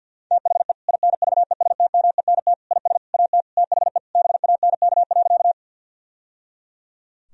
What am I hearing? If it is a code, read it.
Morse code, heard as a keyed tone, about 46 wpm, 700 Hz. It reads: THE INVESTMENT IS AT THE BANK.
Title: Sentences from Top 1000 Words - Review Part 4 - 50wpm